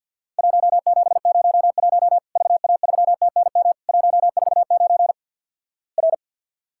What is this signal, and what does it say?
Morse code, heard as a keyed tone, about 50 wpm, 700 Hz. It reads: 1701 VA3TNM 149 R
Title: Worked All Europe DX Contest - QTC Exchange - 50wpm